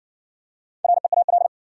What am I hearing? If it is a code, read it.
Morse code, heard as a keyed tone, about 58 wpm, 700 Hz. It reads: REAL